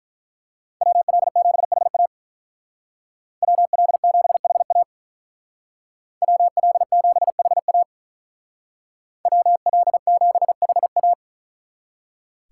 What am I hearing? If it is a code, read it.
Morse code, heard as a keyed tone, about 43 wpm, 700 Hz. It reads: WL7HA WL7HA WL7HA WL7HA
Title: Speed Racing - USA 2x2 Call Signs 35wpm